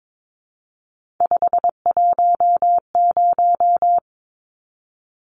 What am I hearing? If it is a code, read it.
Morse code, heard as a keyed tone, about 22 wpm, 700 Hz. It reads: 510
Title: Numbers - 3-Digit 22wpm